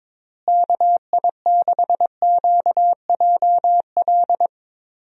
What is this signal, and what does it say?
Morse code, heard as a keyed tone, about 22 wpm, 700 Hz. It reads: KI6QJL
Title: USA Call Signs - 6 Characters - Repeated 2x - 22wpm